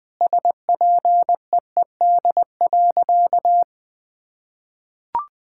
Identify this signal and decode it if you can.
Morse code, heard as a keyed tone, about 20 wpm, 700 Hz. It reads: SPEED.
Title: Sentences from Top 1000 Words - Review Part 5 - 20wpm